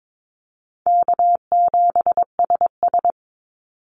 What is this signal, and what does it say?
Morse code, heard as a keyed tone, about 22 wpm, 700 Hz. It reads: K7SS
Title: Call Signs Repeated 2x - 22wpm